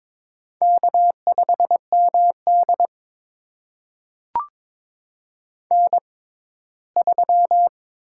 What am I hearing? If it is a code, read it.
Morse code, heard as a keyed tone, about 22 wpm, spaced as for 15 wpm, 700 Hz. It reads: K5MD N3
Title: USA Call Signs - 6x Character Spacing - 22wpm